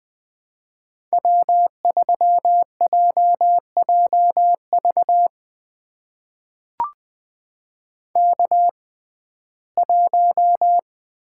Morse code, heard as a keyed tone, about 20 wpm, spaced as for 15 wpm, 700 Hz. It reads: W3JJV K1
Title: USA Call Signs - 5 Characters - 6x Character Spacing - 20wpm